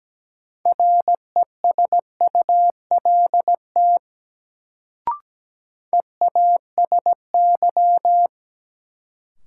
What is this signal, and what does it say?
Morse code, heard as a keyed tone, about 17 wpm, 700 Hz. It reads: RESULT EASY
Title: Sentences from Top 700 Words - New Word Drill - 17wpm